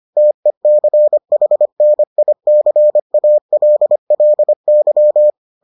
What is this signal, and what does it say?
Morse code, heard as a keyed tone, about 25 wpm, 600 Hz. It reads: TECHNICALLY